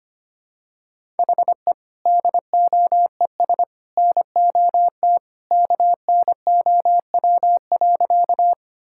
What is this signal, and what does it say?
Morse code, heard as a keyed tone, about 25 wpm, 700 Hz. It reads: HE DOES NOT KNOW.
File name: practice / 25wpm / Sentences with 3-Word Phrases - 25wpm